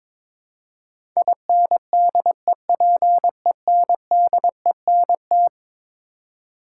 Morse code, heard as a keyed tone, about 21 wpm, 700 Hz. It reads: INDEPENDENT
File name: practice / 22wpm / Eleven Letter Words - 22wpm